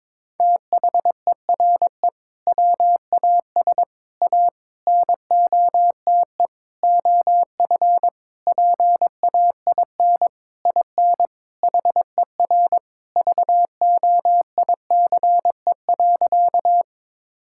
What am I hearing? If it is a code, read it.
Morse code, heard as a keyed tone, about 22 wpm, 700 Hz. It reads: THERE WAS A NOTE OF PAIN IN HER VOICE.
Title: Sentences from Top 1600 Words 22wpm